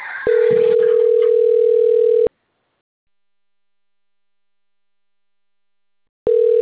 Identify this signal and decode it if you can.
Ringback tone (North America), 440 + 480 Hz.
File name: rg-1600-unknown- - - .42161